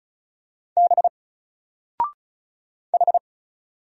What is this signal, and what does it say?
Morse code, heard as a keyed tone, about 35 wpm, 700 Hz. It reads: B H
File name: Single-Letter-Number-35wpm